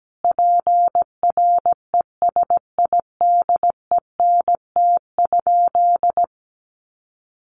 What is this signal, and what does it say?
Morse code, heard as a keyed tone, about 17 wpm, 700 Hz. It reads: PRESIDENT?
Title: Sentences from Top 1300 Words 17wpm